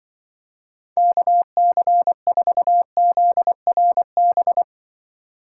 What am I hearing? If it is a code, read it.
Morse code, heard as a keyed tone, about 24 wpm, 700 Hz. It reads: KC4ZRB
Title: Speed Racing - USA 2x3 Call Signs 20wpm